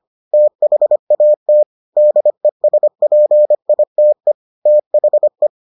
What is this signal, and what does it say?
Morse code, heard as a keyed tone, about 25 wpm, 600 Hz. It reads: THAT DESPITE THE